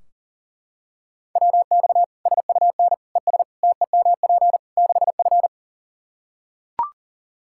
Morse code, heard as a keyed tone, about 40 wpm, 700 Hz. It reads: WX SUN ES TEMP 6F
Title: Practice QSO - Exchange 2 - Info - 40wpm